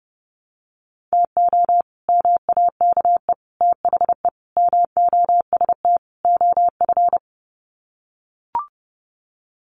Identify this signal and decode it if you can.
Morse code, heard as a keyed tone, about 30 wpm, 700 Hz. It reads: TO MAKE THE MOST OF
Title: Sets of 5 Words Encore2 30wpm